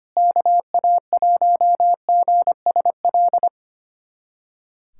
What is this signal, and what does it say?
Morse code, heard as a keyed tone, about 25 wpm, 700 Hz. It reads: KA1GSL
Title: USA Callsign DE Callsign - 6 Characters - 25wpm